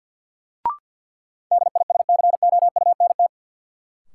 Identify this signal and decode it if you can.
Morse code, heard as a keyed tone, about 53 wpm, 700 Hz. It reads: DISCOUNT
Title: Sentences from Top 1200 Words - New Word Drill - 50wpm